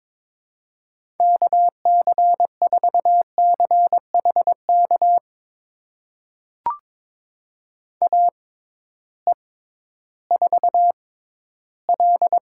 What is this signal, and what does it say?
Morse code, heard as a keyed tone, about 22 wpm, spaced as for 13 wpm, 700 Hz. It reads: KC4CHK AE4L
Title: USA Call Signs - 6x Character Spacing - 22wpm